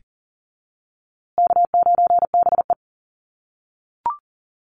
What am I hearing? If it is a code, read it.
Morse code, heard as a keyed tone, about 40 wpm, 700 Hz. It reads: K9BE
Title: USA Call Signs Repeated 2x 40wpm